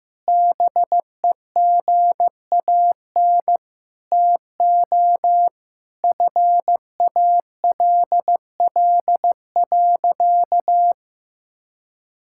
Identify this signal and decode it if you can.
Morse code, heard as a keyed tone, about 15 wpm, 700 Hz. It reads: BEGAN TO FALL.
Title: Sentences from Top 1000 Words - Review Part 6 - 15wpm